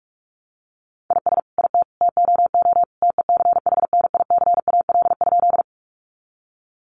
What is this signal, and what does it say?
Morse code, heard as a keyed tone, about 44 wpm, 700 Hz. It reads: IS IT TOO TECHNICAL?
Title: Sentences from Top 1100 Words 45wpm